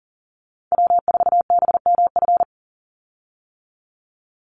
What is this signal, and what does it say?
Morse code, heard as a keyed tone, about 40 wpm, 700 Hz. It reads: W4BMF
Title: USA Call Signs 40wpm